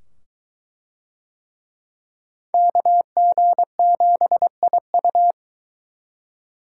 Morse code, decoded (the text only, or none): KG7IU